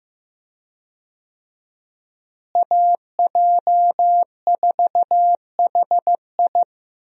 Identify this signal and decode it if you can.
Morse code, heard as a keyed tone, about 16 wpm, 700 Hz. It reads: AJ4HI
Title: USA Call Signs - 5 Characters - Repeated 3x - 15wpm